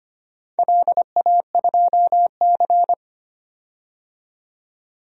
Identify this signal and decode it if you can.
Morse code, heard as a keyed tone, about 25 wpm, 700 Hz. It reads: LA2C
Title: DXCC-100 Call Signs - 4 Characters - 25wpm